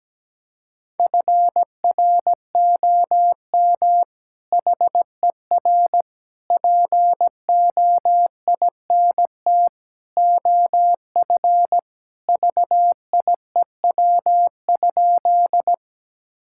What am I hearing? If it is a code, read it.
Morse code, heard as a keyed tone, about 17 wpm, 700 Hz. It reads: FROM HER POINT OF VIEW?
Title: Sentences from Top 1000 Words - Review Part 6 - 17wpm